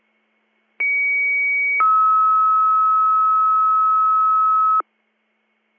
Two-tone page, heard as a one-second tone, then a three-second tone, about 2250 then 1290 Hz. It is heard from 0.8 to 4.8 s.